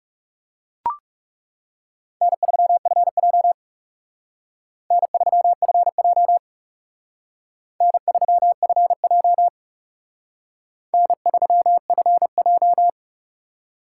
Morse code, heard as a keyed tone, about 39 wpm, 700 Hz. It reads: N3FJ N3FJ N3FJ N3FJ